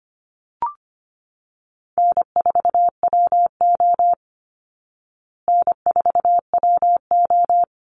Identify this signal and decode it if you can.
Morse code, heard as a keyed tone, about 25 wpm, 700 Hz. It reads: N4WO N4WO
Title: Easy Call Signs Repeated 3x - 25wpm